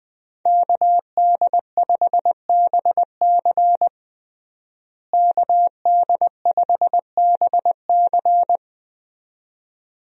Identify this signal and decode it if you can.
Morse code, heard as a keyed tone, about 21 wpm, 700 Hz. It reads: KD5BC KD5BC
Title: USA Call Signs - 5 Characters - Repeated 2x - 20wpm